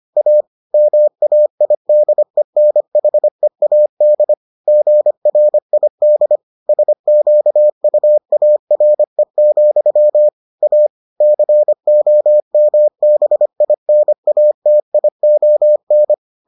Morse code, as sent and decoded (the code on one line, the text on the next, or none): .- / -- .- .. -.. -. .... .- -.. / --. .-. .. -.. / ... --.- ..- .- .-. --..-- / .- / -.-. --- -- -... .. -. .- - .. --- -.
A MAIDENHEAD GRID SQUARE, A COMBINATION